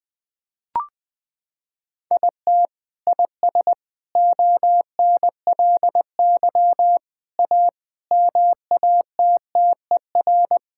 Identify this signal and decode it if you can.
Morse code, heard as a keyed tone, about 20 wpm, 700 Hz. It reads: IT IS ONLY A MATTER
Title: Sentences with 8-Word Phrases - 20wpm